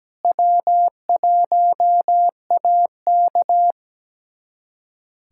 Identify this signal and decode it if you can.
Morse code, heard as a keyed tone, about 17 wpm, 700 Hz. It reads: W1AK
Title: USA Callsign DE Callsign - 4 Characters - 17wpm